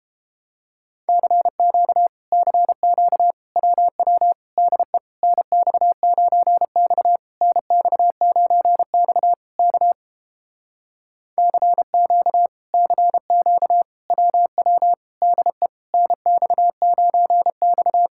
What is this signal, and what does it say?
Morse code, heard as a keyed tone, about 31 wpm, 700 Hz. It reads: CQ CQ WW DE NX9X NX9X K CQ CQ WW DE NX9X